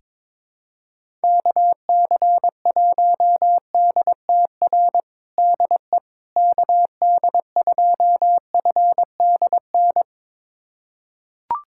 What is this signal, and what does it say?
Morse code, heard as a keyed tone, about 22 wpm, 700 Hz. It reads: KC1DTR DE KD2FDN
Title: USA Callsign DE Callsign 22wpm